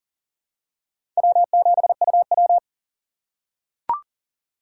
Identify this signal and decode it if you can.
Morse code, heard as a keyed tone, about 37 wpm, 700 Hz. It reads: W7UW
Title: USA Call Signs - 4 Characters - 2x Character Spacing - 40wpm